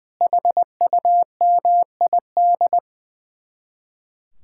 Morse code, heard as a keyed tone, about 20 wpm, 700 Hz. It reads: HUMID
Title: Speed-Racing Common QSO Phrases - 20wpm